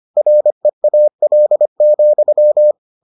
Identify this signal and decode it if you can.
Morse code, heard as a keyed tone, about 25 wpm, 600 Hz. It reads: REAL,